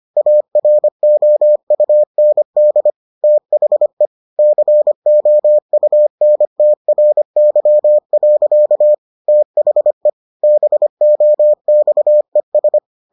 Morse code, heard as a keyed tone, about 25 wpm, 600 Hz. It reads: AROUND THE COUNTRY. THE BOXES